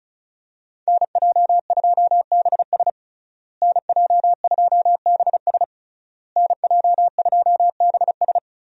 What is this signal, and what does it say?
Morse code, heard as a keyed tone, about 36 wpm, 700 Hz. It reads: NJ2BS NJ2BS NJ2BS
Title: USA Call Signs - 5 Characters - Repeated 3x - 35wpm